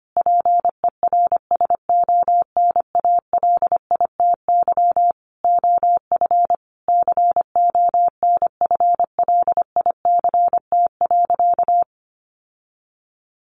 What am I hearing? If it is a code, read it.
Morse code, heard as a keyed tone, about 25 wpm, 700 Hz. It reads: PERSONALITY OF CONFLICT.
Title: Sentences from Top 1400 Words 25wpm